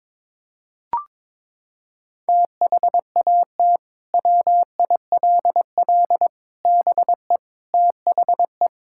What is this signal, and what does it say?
Morse code, heard as a keyed tone, about 22 wpm, 700 Hz. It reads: THAT WILL BE THE
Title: Sentences from Top 1000 Words - Review Part 6 - 22wpm